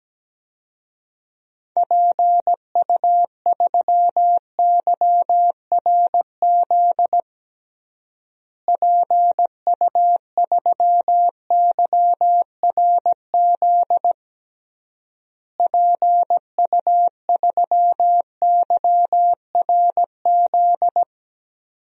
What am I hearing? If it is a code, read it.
Morse code, heard as a keyed tone, about 17 wpm, 700 Hz. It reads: PU3YRZ PU3YRZ PU3YRZ
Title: DXCC-100 Call Signs - 6 Characters - Repeated 3x - 17wpm